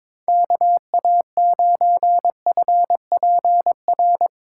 Morse code, heard as a keyed tone, about 22 wpm, 700 Hz. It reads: KA9FPR
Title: USA Call Signs - 6 Characters - 4x Character Spacing - 22wpm